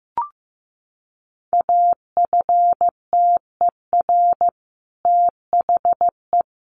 Morse code, heard as a keyed tone, about 15 wpm, 700 Hz. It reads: AFTER THE
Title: A Sound Of Thunder 15wpm